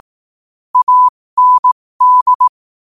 Morse code, heard as a keyed tone, about 18 wpm, spaced as for 15 wpm, 1000 Hz. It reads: AND
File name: qotdfiles / 15wpm